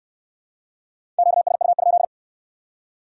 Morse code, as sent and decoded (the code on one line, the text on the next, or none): --.- ... ..- ..--..
QSU?